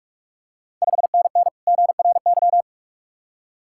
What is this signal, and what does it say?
Morse code, heard as a keyed tone, about 45 wpm, 690 Hz. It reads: 5NN GRY
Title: Ontario QSO Party Exchange 45wpm